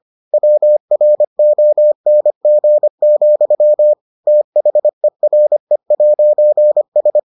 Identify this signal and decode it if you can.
Morse code, heard as a keyed tone, about 25 wpm, 600 Hz. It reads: WRONG, THERE'S